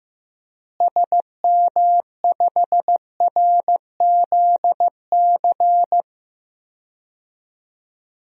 Morse code, heard as a keyed tone, about 15 wpm, 700 Hz. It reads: SM5RZC
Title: DXCC-100 Call Signs - 6 Characters - Repeated 2x - 15wpm